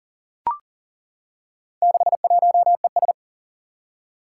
Morse code, heard as a keyed tone, about 40 wpm, 700 Hz. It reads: 61ES
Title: Four Challenging Letter-Number Combinations 40wpm